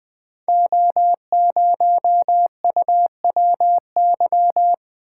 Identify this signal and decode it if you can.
Morse code, heard as a keyed tone, about 21 wpm, 700 Hz. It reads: O0UWY
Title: Code Groups - 5 Letter:Numbers - 20wpm